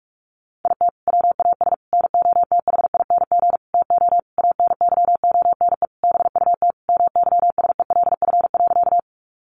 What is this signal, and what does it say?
Morse code, heard as a keyed tone, about 44 wpm, 700 Hz. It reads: IT WAS NOTHING TO ANYONE BUT MYSELF.
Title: Sentences from Top 1300 Words 45wpm